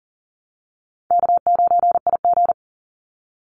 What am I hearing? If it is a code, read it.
Morse code, heard as a keyed tone, about 39 wpm, 700 Hz. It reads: K9IG
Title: Easy Call Signs Repeated 3x - 40wpm